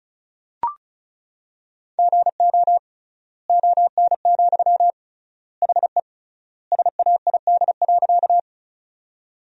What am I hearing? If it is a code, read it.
Morse code, heard as a keyed tone, about 36 wpm, 700 Hz. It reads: GO ON, HE SAID.
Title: 3x Word-Spacing - Sentences from Top 100 Words - 35wpm